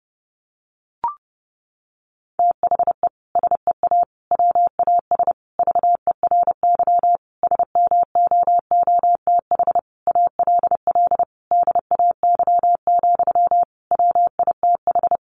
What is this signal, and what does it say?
Morse code, heard as a keyed tone, about 30 wpm, 700 Hz. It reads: THE SEA WAS VERY SMOOTH ALL DAY, WITH
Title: Sentences from Top 1400 Words 30wpm